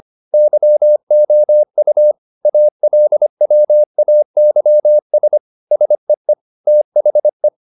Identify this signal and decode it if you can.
Morse code, heard as a keyed tone, about 25 wpm, 600 Hz. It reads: YOU ALWAYS SEE THE
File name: the old man and the sea part 1 25wpm